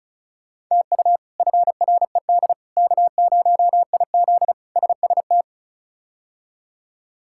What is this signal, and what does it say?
Morse code, heard as a keyed tone, about 36 wpm, 700 Hz. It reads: TU FRED K0IZ SST
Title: K1USN SST QSO Practice - 35wpm